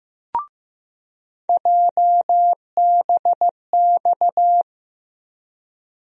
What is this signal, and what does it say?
Morse code, heard as a keyed tone, about 15 wpm, 700 Hz. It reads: JBX
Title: USA Call Sign Postfixes - 3 Letters 15wpm